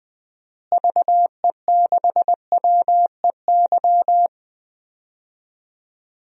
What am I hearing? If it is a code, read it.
Morse code, heard as a keyed tone, about 20 wpm, 700 Hz. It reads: VE6WEY
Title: DXCC-100 Call Signs - 6 Characters - Repeated 2x - 20wpm